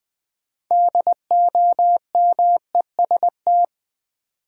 Morse code, heard as a keyed tone, about 20 wpm, 700 Hz. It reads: DOMEST
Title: Build A Word - 2 Letters At A Time - 8-Letter Words - 20wpm